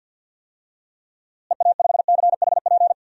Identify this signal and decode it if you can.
Morse code, heard as a keyed tone, about 47 wpm, 700 Hz. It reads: EA5CHP
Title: DXCC-100 Call Signs - 6 Characters - Repeated 2x - 50wpm